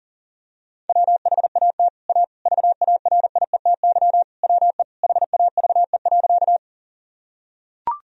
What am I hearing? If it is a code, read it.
Morse code, heard as a keyed tone, about 40 wpm, 700 Hz. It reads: WHAT A VARIETY WE HAVE.
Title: Sentences from Top 1000 Words - Review Part 4 - 40wpm